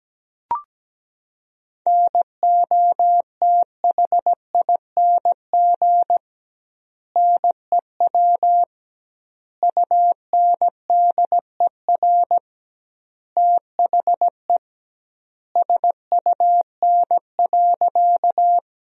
Morse code, heard as a keyed tone, about 17 wpm, 700 Hz. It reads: NOTHING NEW UNDER THE SUN.
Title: 2x Word-Spacing - Sentences from Top 500 Words - 17wpm